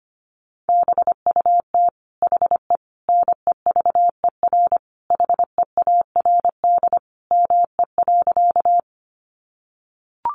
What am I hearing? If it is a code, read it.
Morse code, heard as a keyed tone, about 25 wpm, 700 Hz. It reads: BUT HE NEVER HEARD ME.
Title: Sentences from Top 400 Words 25wpm